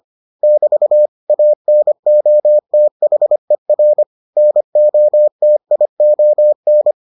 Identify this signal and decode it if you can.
Morse code, heard as a keyed tone, about 25 wpm, 600 Hz. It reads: <BT> ANOTHER NOTION